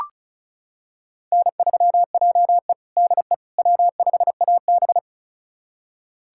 Morse code, heard as a keyed tone, about 36 wpm, 700 Hz. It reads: N3JE DE W5AB
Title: USA Callsign DE Callsign - 4 Characters - 35wpm